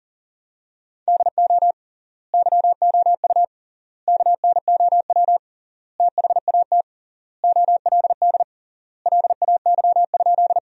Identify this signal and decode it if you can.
Morse code, heard as a keyed tone, about 40 wpm, 700 Hz. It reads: DO YOU KNOW THAT OLD LAY?
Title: 3x Word-Spacing - Sentences from Top 400 Words - 40wpm